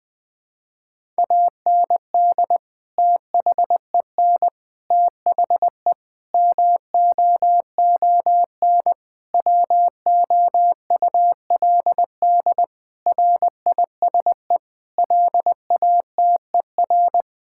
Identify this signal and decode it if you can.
Morse code, heard as a keyed tone, about 20 wpm, 700 Hz. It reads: AND THEN THE MOON WOULD RISE LATER
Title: Sentences from Top 1000 Words - Review Part 6 - 20wpm